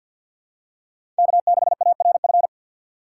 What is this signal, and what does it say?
Morse code, heard as a keyed tone, about 46 wpm, 700 Hz. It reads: K6ARF
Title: USA Call Signs - 5 Characters - 3x Character Spacing - 50wpm